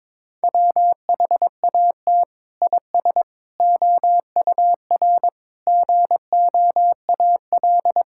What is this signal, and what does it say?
Morse code, heard as a keyed tone, about 22 wpm, 700 Hz. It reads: WHAT IS OUR GOAL
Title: Sentences from Top 1000 Words - Review Part 5 - 22wpm